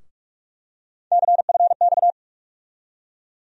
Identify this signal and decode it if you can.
Morse code, heard as a keyed tone, about 47 wpm, 700 Hz. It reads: CFX
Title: USA Call Sign Postfixes - 3 Letters 45wpm